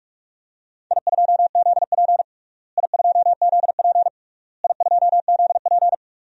Morse code, heard as a keyed tone, about 46 wpm, 700 Hz. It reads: I2ZP I2ZP I2ZP
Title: DXCC-100 Call Signs - 4 Characters - Repeated 3x - 45wpm